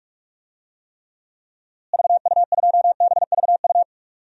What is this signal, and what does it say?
Morse code, heard as a keyed tone, about 46 wpm, 700 Hz. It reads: VU2BVU
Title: DXCC-100 Call Signs - 6 Characters - Repeated 3x - 45wpm